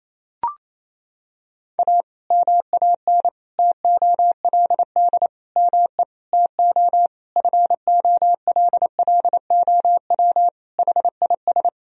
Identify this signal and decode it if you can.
Morse code, heard as a keyed tone, about 28 wpm, 700 Hz. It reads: A MAN TOLD ME TO FOLLOW HIS
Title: Sentences from Top 1000 Words - Review Part 3 - 28wpm